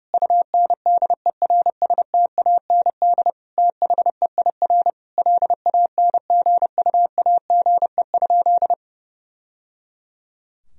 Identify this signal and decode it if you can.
Morse code, heard as a keyed tone, about 30 wpm, 700 Hz. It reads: UNDERSTAND THEIR LANGUAGE?
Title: Sentences from Top 1000 Words - Review Part 3 - 30wpm